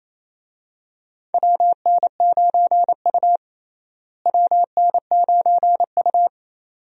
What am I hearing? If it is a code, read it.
Morse code, heard as a keyed tone, about 28 wpm, 700 Hz. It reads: WN9U WN9U
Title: USA Call Signs - 4 Characters - Repeated 2x - 28wpm